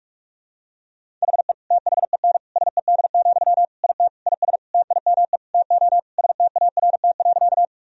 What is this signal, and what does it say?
Morse code, heard as a keyed tone, about 46 wpm, 700 Hz. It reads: HE THEN SED, IT IS TIME TO START.
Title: Sentences from Top 300 Words and Abbreviations 45wpm